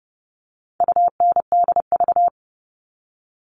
Morse code, heard as a keyed tone, about 30 wpm, 700 Hz. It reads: UNDV
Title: All Cut Numbers - 4-Digit 30wpm